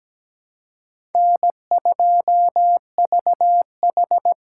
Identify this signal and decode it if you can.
Morse code, heard as a keyed tone, about 17 wpm, 700 Hz. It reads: N2VH
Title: USA Call Signs - 2x Character Spacing - 17wpm